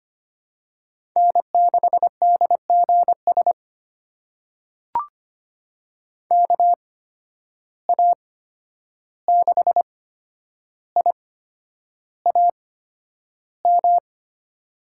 Morse code, heard as a keyed tone, about 25 wpm, spaced as for 10 wpm, 700 Hz. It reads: N6DGS KA6IAM
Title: USA Call Signs - 8x Character Spacing - 25wpm